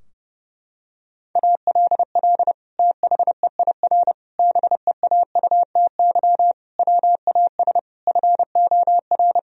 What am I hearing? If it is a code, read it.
Morse code, heard as a keyed tone, about 30 wpm, 700 Hz. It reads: ALL THEIR BEAUTY WAS FOR
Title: Sentences from Top 1000 Words - Review Part 2 - 30wpm